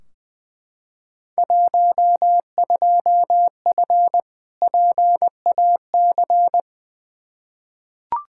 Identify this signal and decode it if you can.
Morse code, heard as a keyed tone, about 20 wpm, 700 Hz. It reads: 12F PAC
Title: Field Day Exchanges Encore - 20wpm